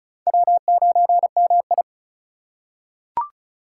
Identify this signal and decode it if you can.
Morse code, heard as a keyed tone, about 36 wpm, 700 Hz. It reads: W9MI
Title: USA Call Signs - 4 Characters - 4x Character Spacing - 35wpm